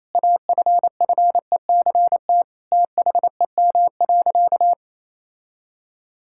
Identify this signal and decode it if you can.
Morse code, heard as a keyed tone, about 28 wpm, 700 Hz. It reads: AFFECT THEM.
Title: Sentences from Top 1000 Words - Review Part 6 - 28wpm